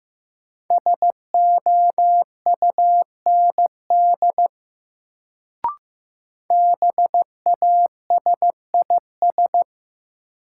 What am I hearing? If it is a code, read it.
Morse code, heard as a keyed tone, about 15 wpm, 700 Hz. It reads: SOUND BASIS
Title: Sets of 3 Words - New Word Drill - 15wpm